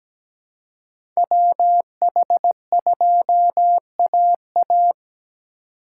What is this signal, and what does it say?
Morse code, heard as a keyed tone, about 17 wpm, 700 Hz. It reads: WH2AA
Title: USA Call Signs - 5 Characters - 2x Character Spacing - 17wpm